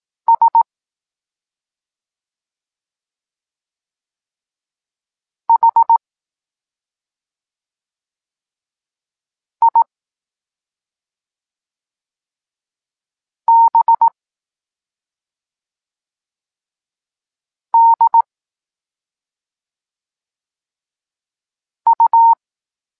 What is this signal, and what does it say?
Morse code, heard as a keyed tone, about 18 wpm, 920 Hz. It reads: S H I B D U
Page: Tekst 14 – doskonalenie odbioru poznanych liter